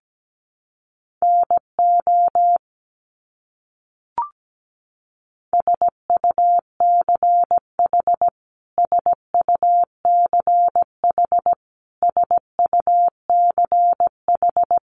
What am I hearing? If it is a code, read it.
Morse code, heard as a keyed tone, about 17 wpm, 700 Hz. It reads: NO SUCH SUCH SUCH
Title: Top 200 Words Repeated 17wpm